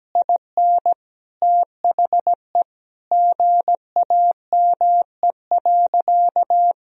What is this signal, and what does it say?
Morse code, heard as a keyed tone, about 17 wpm, 700 Hz. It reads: IN THE GAME.
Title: Sentences from Top 1000 Words - Review Part 4 - 17wpm